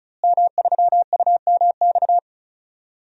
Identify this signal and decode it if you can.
Morse code, heard as a keyed tone, about 36 wpm, 700 Hz. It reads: M3UMX
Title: DXCC-100 Call Signs - 5 Characters - Repeated 3x - 35wpm